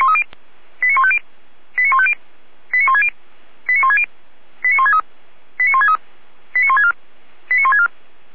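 Five-tone selective call (ZVEI), heard as quick runs of five tones, tones of about 70 ms. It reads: 89150 89150 89160 89160 89163 89163 89164 89164
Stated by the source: ZVEI five-tone call 89121 received on SDR.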